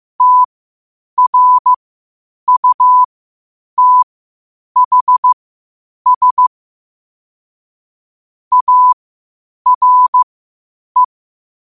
Morse code, heard as a keyed tone, about 15 wpm, spaced as for 8 wpm, 1000 Hz. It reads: TRUTHS ARE
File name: qotdfiles / 8wpm